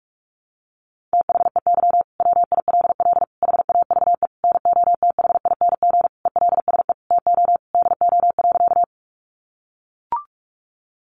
Morse code, heard as a keyed tone, about 44 wpm, 700 Hz. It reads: THEY WILL HAVE NOTHING ELSE TO DO.
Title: Sentences from Top 800 Words 45wpm